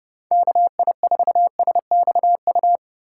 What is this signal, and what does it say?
Morse code, heard as a keyed tone, about 30 wpm, 700 Hz. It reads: KI4SXU
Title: USA Call Signs - 6 Characters - 4x Character Spacing - 30wpm